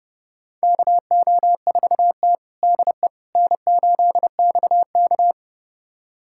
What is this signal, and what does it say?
Morse code, heard as a keyed tone, about 30 wpm, 700 Hz. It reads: KO4T DE N8XK
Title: USA Callsign DE Callsign - 4 Characters - 30wpm